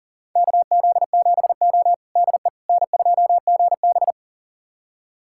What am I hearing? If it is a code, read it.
Morse code, heard as a keyed tone, about 43 wpm, 690 Hz. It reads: KZ7O DE N2GB
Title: USA Callsign DE Callsign - 4 Characters - 40wpm